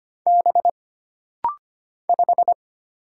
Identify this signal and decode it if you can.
Morse code, heard as a keyed tone, about 25 wpm, 700 Hz. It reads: B 5
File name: Single Character (Letter:Number:Punctuation:Prosign) - Rapid-Fire - 25wpm